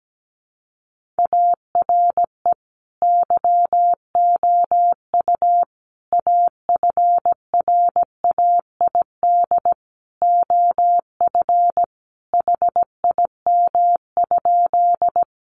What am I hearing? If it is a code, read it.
Morse code, heard as a keyed tone, about 17 wpm, 700 Hz. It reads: ARE YOU AFRAID OF HIM?
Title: Sentences from Top 1500 Words 17wpm